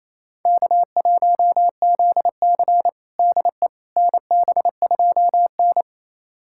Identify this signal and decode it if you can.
Morse code, heard as a keyed tone, about 28 wpm, 690 Hz. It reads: K1ZC DE NB2N
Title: USA Callsign DE Callsign - 4 Characters - 28wpm